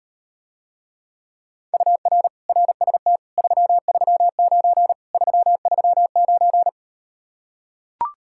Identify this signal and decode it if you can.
Morse code, heard as a keyed tone, about 39 wpm, 700 Hz. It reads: UR RST 339 339